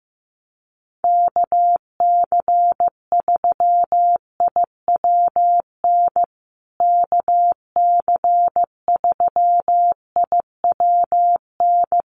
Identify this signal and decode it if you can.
Morse code, heard as a keyed tone, about 15 wpm, 700 Hz. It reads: KC3IWN KC3IWN
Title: USA Call Signs Repeated 2x 15wpm